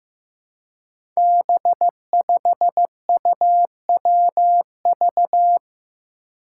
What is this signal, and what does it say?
Morse code, heard as a keyed tone, about 15 wpm, 700 Hz. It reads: B5UWV
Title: Five Challenging Letter-Number Combinations 15wpm